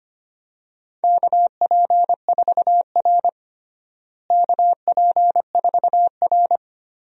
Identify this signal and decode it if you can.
Morse code, heard as a keyed tone, about 25 wpm, 700 Hz. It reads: KP4R KP4R
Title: DXCC-100 Call Signs - 4 Characters - Repeated 3x - 25wpm